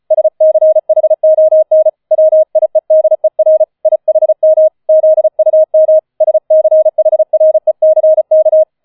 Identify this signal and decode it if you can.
Morse code, heard as a keyed tone, about 35 wpm, 630 Hz. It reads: SCHON WIEDER IHM ZUM SCHRECK